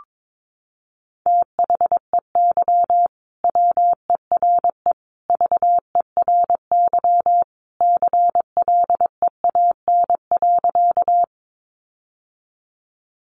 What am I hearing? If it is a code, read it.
Morse code, heard as a keyed tone, about 22 wpm, 700 Hz. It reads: THEY WERE VERY CLEAN.
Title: Sentences from Top 1100 Words 22wpm